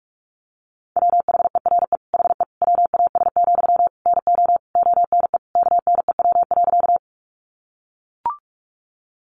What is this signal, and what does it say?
Morse code, heard as a keyed tone, about 44 wpm, 700 Hz. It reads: WHERE HE WAS, NO ONE KNEW.
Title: Sentences from Top 400 Words 45wpm